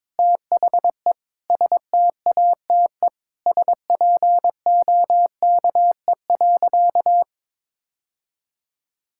Morse code, heard as a keyed tone, about 22 wpm, 700 Hz. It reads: THE STATE SPOKE.
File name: practice / 22wpm / Sentences with 3-Word Phrases - 22wpm